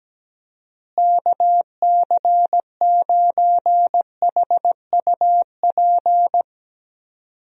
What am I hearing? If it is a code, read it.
Morse code, heard as a keyed tone, about 17 wpm, 700 Hz. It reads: KC9HUP
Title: USA Call Signs - 6 Characters - Repeated 3x - 17wpm